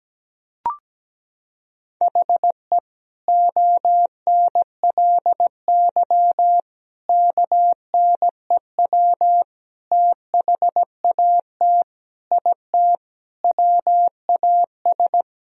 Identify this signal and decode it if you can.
Morse code, heard as a keyed tone, about 17 wpm, 700 Hz. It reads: HE ONLY KNEW THAT IT WAS
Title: Sentences from Top 1000 Words - Review Part 2 - 17wpm